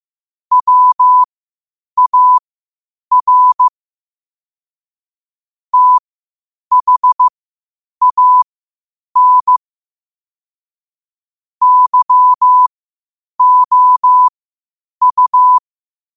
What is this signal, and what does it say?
Morse code, heard as a keyed tone, about 15 wpm, spaced as for 8 wpm, 1000 Hz. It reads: WAR THAN YOU
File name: 8wpm